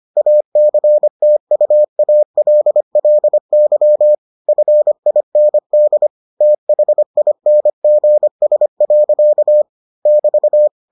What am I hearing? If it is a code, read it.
Morse code, heard as a keyed tone, about 25 wpm, 600 Hz. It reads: ACTUALLY FIND THINGS. <BT>